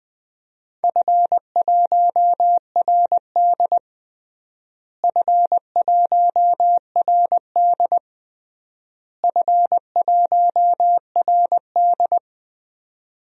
Morse code, heard as a keyed tone, about 20 wpm, 700 Hz. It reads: F1RD F1RD F1RD
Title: DXCC-100 Call Signs - 4 Characters - Repeated 3x - 20wpm